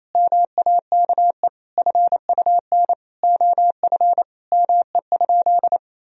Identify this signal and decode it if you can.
Morse code, heard as a keyed tone, about 28 wpm, 700 Hz. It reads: MAKE FUN OF ME?
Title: Sentences from Top 1000 Words - Review Part 4 - 28wpm